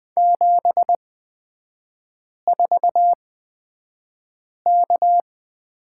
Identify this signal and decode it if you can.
Morse code, heard as a keyed tone, about 20 wpm, spaced as for 5 wpm, 700 Hz. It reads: 74K
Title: 255 Lesson 41e - Period - Three Characters 20wpm